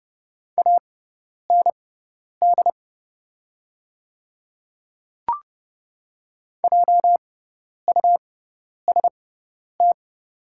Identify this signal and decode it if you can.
Morse code, heard as a keyed tone, about 30 wpm, spaced as for 10 wpm, 690 Hz. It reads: AND JUST
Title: 6x Character Spacing - Top 200 Words - 30wpm